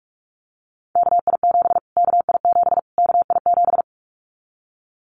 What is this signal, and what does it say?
Morse code, heard as a keyed tone, about 44 wpm, 700 Hz. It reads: KI7 KI7 KI7
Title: Callsign Prefixes for Top 25 DXCC - Repeated 3x - 45wpm